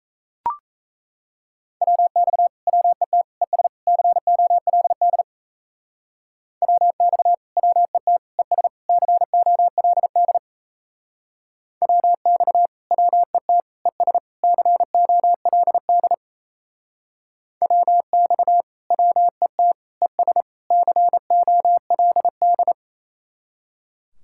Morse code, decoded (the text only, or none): WX WET ES COLD WX WET ES COLD WX WET ES COLD WX WET ES COLD